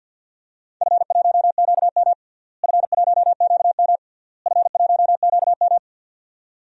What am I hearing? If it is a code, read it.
Morse code, heard as a keyed tone, about 52 wpm, 700 Hz. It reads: F1QM F1QM F1QM
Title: DXCC-100 Call Signs - 4 Characters - Repeated 3x - 50wpm